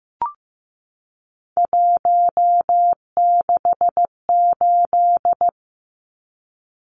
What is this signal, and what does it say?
Morse code, heard as a keyed tone, about 15 wpm, 700 Hz. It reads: 168
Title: Numbers - 3-Digit 15wpm